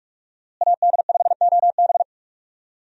Morse code, heard as a keyed tone, about 46 wpm, 700 Hz. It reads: AD5OB
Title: USA Call Signs - 5 Characters - Repeated 2x - 45wpm